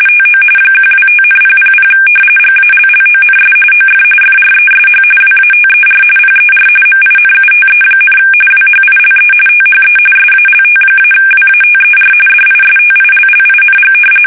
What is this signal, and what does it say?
RTTY, about 45.45 baud. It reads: TO 125,000 GALLONS OF FRESH WATER PER MINUTE, WHILE THE SALT-WATER DISTILLING
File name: fsk850 / battleshipnj